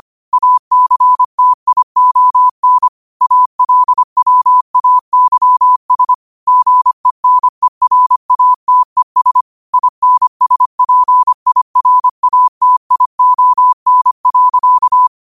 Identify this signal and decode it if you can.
Morse code, heard as a keyed tone, about 25 wpm, 1000 Hz. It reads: ACTION ALWAYS GENERATES INSPIRATION.
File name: qotdfiles / 25wpm